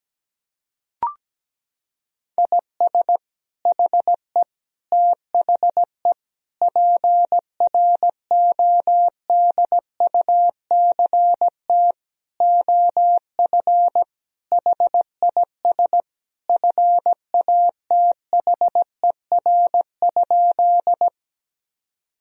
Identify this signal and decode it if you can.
Morse code, heard as a keyed tone, about 17 wpm, 700 Hz. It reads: IS HE THE PRODUCT OF HIS FATHER?
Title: Sentences from Top 1000 Words - Review Part 2 - 17wpm